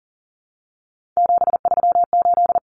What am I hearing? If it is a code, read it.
Morse code, heard as a keyed tone, about 40 wpm, 700 Hz. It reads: 738
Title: Common Cut Numbers - 3-Digit 40wpm